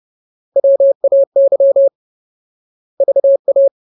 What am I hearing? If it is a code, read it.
Morse code, heard as a keyed tone, about 30 wpm, 550 Hz. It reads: WAY VA
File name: Ultimate Word List / 30WPM Ultimate